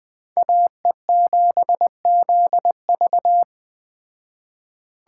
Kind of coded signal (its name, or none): Morse code